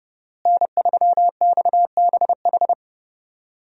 Morse code, heard as a keyed tone, about 30 wpm, 700 Hz. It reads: N3XBH
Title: USA Call Signs - 5 Characters - Repeated 2x - 30wpm